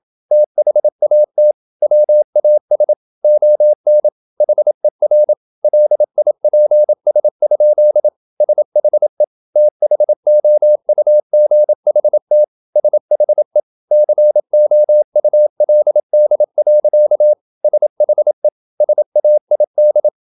Morse code, heard as a keyed tone, about 27 wpm, 600 Hz. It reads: THAT WAS ON HER LIPS? SHE THOUGHT SHE COULD. SHE SAID
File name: Never 0027wpm